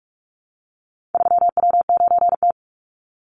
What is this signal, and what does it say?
Morse code, heard as a keyed tone, about 44 wpm, 700 Hz. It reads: 3W9T
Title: Call Signs - 45wpm